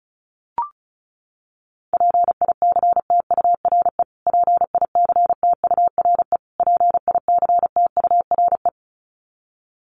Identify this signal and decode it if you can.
Morse code, heard as a keyed tone, about 34 wpm, 700 Hz. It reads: PICTURE PICTURE PICTURE
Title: Top 500 Words Repeated 35wpm